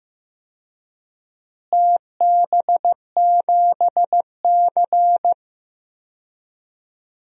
Morse code, heard as a keyed tone, about 15 wpm, 700 Hz. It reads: TB7C
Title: DXCC-100 Call Signs - 4 Characters - Repeated 3x - 15wpm